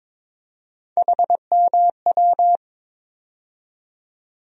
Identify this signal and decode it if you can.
Morse code, heard as a keyed tone, about 21 wpm, 700 Hz. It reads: HMW
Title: USA Call Sign Postfixes - 3 Letters 22wpm